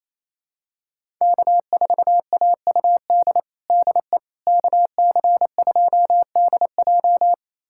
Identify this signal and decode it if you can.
Morse code, heard as a keyed tone, about 28 wpm, 700 Hz. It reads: K4AUD DE KC2DJ
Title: USA Callsign DE Callsign - 5 Characters - 28wpm